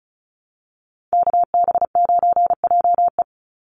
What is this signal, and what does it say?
Morse code, heard as a keyed tone, about 34 wpm, 700 Hz. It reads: KB9JE
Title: USA Call Signs 35wpm